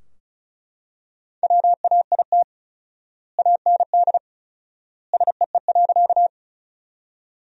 Morse code, heard as a keyed tone, about 36 wpm, 700 Hz. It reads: WAIT AND SEE.
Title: 4x Word-Spacing - Sentences from Top 500 Words - 35wpm